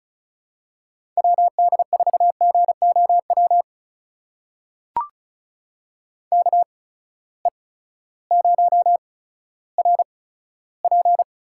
Morse code, heard as a keyed tone, about 36 wpm, spaced as for 16 wpm, 700 Hz. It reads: WD4GOW KE0RP
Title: USA Call Signs - 6 Characters - 8x Character Spacing - 35wpm